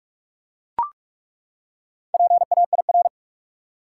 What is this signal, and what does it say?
Morse code, heard as a keyed tone, about 41 wpm, 700 Hz. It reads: PAIR
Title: Four Letter Words - 45wpm